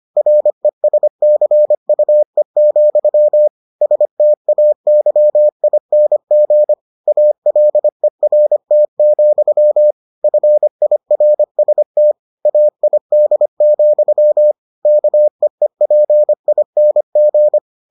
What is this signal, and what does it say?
Morse code, heard as a keyed tone, about 25 wpm, 600 Hz. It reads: RESCUE, STAYING ALERT, FIRST AID, KEEPING